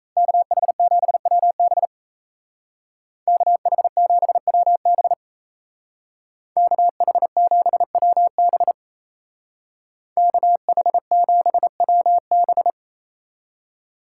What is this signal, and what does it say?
Morse code, heard as a keyed tone, about 37 wpm, 700 Hz. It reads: KH7WB KH7WB KH7WB KH7WB